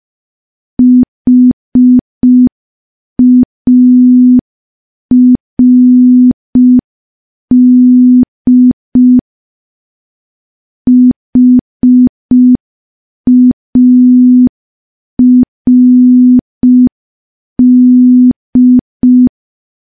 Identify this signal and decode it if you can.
Morse code, heard as a keyed tone, about 5 wpm, 250 Hz. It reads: HARD HARD
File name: morse / morse-training-05WPM